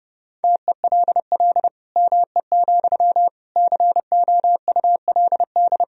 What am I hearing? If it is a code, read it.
Morse code, heard as a keyed tone, about 30 wpm, 700 Hz. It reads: TELL ME, COULD
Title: Sentences from Top 1000 Words - Review Part 2 - 30wpm